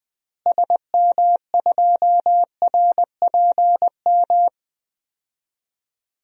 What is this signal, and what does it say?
Morse code, heard as a keyed tone, about 20 wpm, 700 Hz. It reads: SM2RPM
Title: DXCC-100 Call Signs - 6 Characters - Repeated 3x - 20wpm